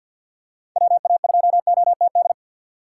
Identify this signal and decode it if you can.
Morse code, heard as a keyed tone, about 51 wpm, 700 Hz. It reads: WA2OTD
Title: USA Call Signs - 6 Characters - 4x Character Spacing - 50wpm